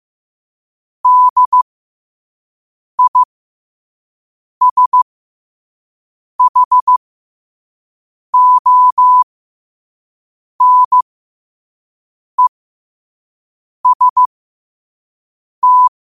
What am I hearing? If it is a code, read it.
Morse code, heard as a keyed tone, about 15 wpm, spaced as for 5 wpm, 1000 Hz. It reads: DISHONEST
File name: qotdfiles / 5wpm